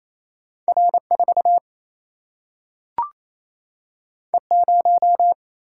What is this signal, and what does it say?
Morse code, heard as a keyed tone, about 28 wpm, 700 Hz. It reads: R 4 E 0
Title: Single-Letter Single-Number 28wpm